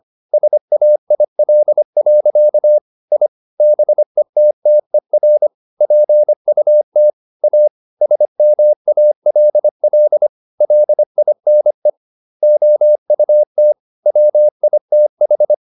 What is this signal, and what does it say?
Morse code, heard as a keyed tone, about 25 wpm, 600 Hz. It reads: SAIL. I BETTER PUT A SMALL LINE OUT WITH